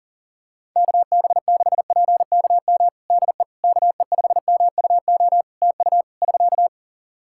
Morse code, heard as a keyed tone, about 40 wpm, 700 Hz. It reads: KB6PKM DE KE5MUO TU <SK>
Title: Practice QSO - Exchange 4 - Ending - 40wpm